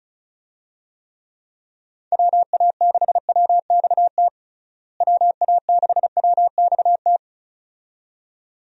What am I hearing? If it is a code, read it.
Morse code, heard as a keyed tone, about 36 wpm, 700 Hz. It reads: WA6WXT WA6WXT
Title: USA Call Signs - 6 Characters - Repeated 2x - 35wpm